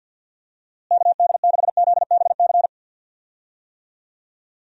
Morse code, heard as a keyed tone, about 47 wpm, 700 Hz. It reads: KD6ZBC
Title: USA Call Signs - 6 Characters - Repeated 3x - 50wpm